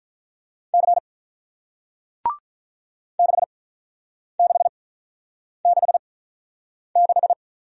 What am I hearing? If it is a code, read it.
Morse code, heard as a keyed tone, about 48 wpm, spaced as for 9 wpm, 700 Hz. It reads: C 6666